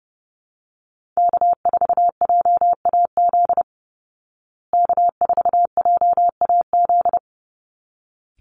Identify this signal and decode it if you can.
Morse code, heard as a keyed tone, about 30 wpm, 700 Hz. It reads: K4JAZ K4JAZ
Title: Call Signs Repeated 3x - 30wpm